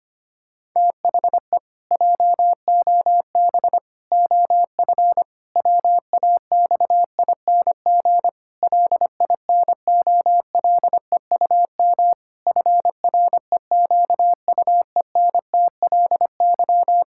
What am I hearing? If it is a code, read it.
Morse code, heard as a keyed tone, about 25 wpm, 700 Hz. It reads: THE JOB OF WAXING LINOLEUM FREQUENTLY
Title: Pangram Challenge v2 - 25wpm